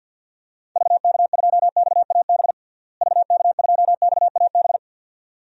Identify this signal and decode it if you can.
Morse code, heard as a keyed tone, about 52 wpm, 700 Hz. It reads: VK2XAB VK2XAB
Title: DXCC-100 Call Signs - 6 Characters - Repeated 2x - 50wpm